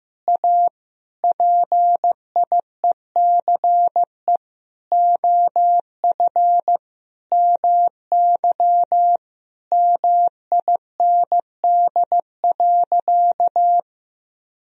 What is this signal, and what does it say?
Morse code, heard as a keyed tone, about 15 wpm, 700 Hz. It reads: A PIECE OF MY MIND.
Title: Sentences from Top 1000 Words - Review Part 2 - 15wpm